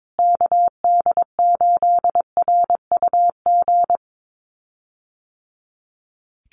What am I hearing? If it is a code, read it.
Morse code, heard as a keyed tone, about 22 wpm, 700 Hz. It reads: KD8RUG
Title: USA Call Signs Repeated 2x 22wpm